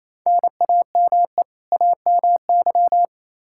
Morse code, heard as a keyed tone, about 28 wpm, 700 Hz. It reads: NAME AMY
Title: Practice QSO - Exchange 1 - Beginning - 28wpm